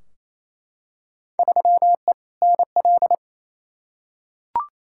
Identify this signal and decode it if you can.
Morse code, heard as a keyed tone, about 28 wpm, 700 Hz. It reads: 3E NL
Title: Field Day Exchanges Encore - 28wpm